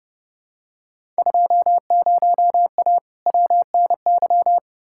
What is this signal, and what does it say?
Morse code, heard as a keyed tone, about 31 wpm, 700 Hz. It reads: 20A WNY
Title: Field Day Exchanges Encore - 30wpm